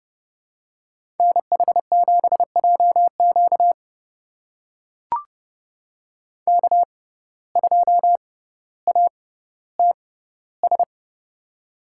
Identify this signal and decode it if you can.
Morse code, heard as a keyed tone, about 31 wpm, spaced as for 15 wpm, 700 Hz. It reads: NH7JQ K2ATS
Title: USA Call Signs - 5 Characters - 6x Character Spacing - 30wpm